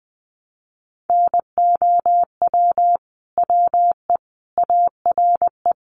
Morse code, heard as a keyed tone, about 20 wpm, 700 Hz. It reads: NOW WE ARE
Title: Sentences from Top 1200 Words 20wpm